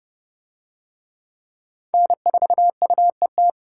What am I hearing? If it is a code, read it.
Morse code, heard as a keyed tone, about 28 wpm, 700 Hz. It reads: N4UET